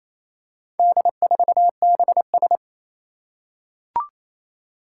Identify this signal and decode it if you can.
Morse code, heard as a keyed tone, about 27 wpm, 700 Hz. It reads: D4BS
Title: Four Challenging Letter-Number Combinations 28wpm